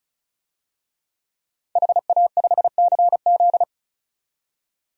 Morse code, heard as a keyed tone, about 34 wpm, 700 Hz. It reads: HA5CZ
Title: DXCC-100 Call Signs - 5 Characters - 35wpm